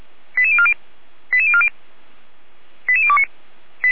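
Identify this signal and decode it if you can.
Five-tone selective call (ZVEI), heard as quick runs of five tones, tones of about 70 ms. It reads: 88040 88040 88029
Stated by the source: Five-tone ZVEI selective call received by SDR.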